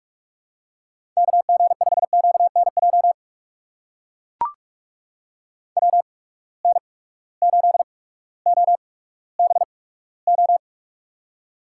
Morse code, heard as a keyed tone, about 44 wpm, spaced as for 19 wpm, 700 Hz. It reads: KG5QNJ WN8OBO
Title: USA Call Signs - 8x Character Spacing - 45wpm